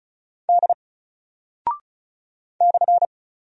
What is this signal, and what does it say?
Morse code, heard as a keyed tone, about 36 wpm, 700 Hz. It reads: D /
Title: Single Character (Letter:Number:Punctuation:Prosign) 35wpm